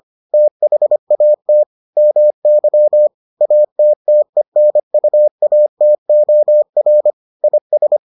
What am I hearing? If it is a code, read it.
Morse code, heard as a keyed tone, about 25 wpm, 600 Hz. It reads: THAT MY ATTENUATOR IS